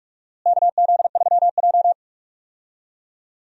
Morse code, heard as a keyed tone, about 46 wpm, 700 Hz. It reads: KZ3J
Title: USA Call Signs - 4 Characters - Repeated 3x - 45wpm